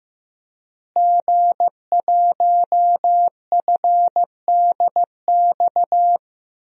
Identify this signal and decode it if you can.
Morse code, heard as a keyed tone, about 15 wpm, 700 Hz. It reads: G1FDX
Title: DXCC-100 Call Signs - 5 Characters - Repeated 3x - 15wpm